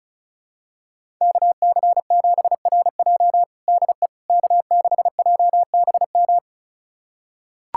Morse code, heard as a keyed tone, about 34 wpm, 700 Hz. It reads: KC7RJ DE K6JBM
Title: USA Callsign DE Callsign - 5 Characters - 35wpm